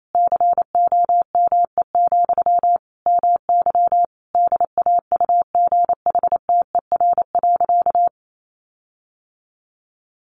Morse code, heard as a keyed tone, about 28 wpm, 700 Hz. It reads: COME, MY DAUGHTER.
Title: Sentences from Top 1500 Words 28wpm